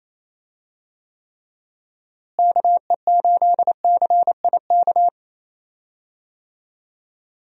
Morse code, heard as a keyed tone, about 28 wpm, 700 Hz. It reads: KE8CIK
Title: USA Call Signs - 6 Characters - Repeated 3x - 28wpm